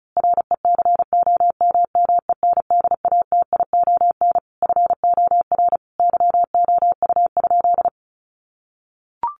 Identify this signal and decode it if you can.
Morse code, heard as a keyed tone, about 34 wpm, 700 Hz. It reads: RECOMMENDATION FOR YOU?
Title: Sentences from Top 1400 Words 35wpm